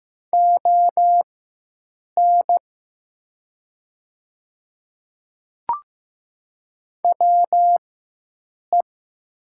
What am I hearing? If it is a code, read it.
Morse code, heard as a keyed tone, about 15 wpm, spaced as for 7 wpm, 700 Hz. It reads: ON WE
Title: 4x Character Spacing - Top 100 Words - 15wpm